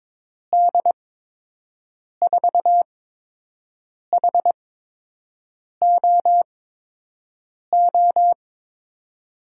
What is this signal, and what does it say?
Morse code, heard as a keyed tone, about 22 wpm, spaced as for 6 wpm, 700 Hz. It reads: D4HOO
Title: USA Call Signs - 6 Characters - 8x Character Spacing - 22wpm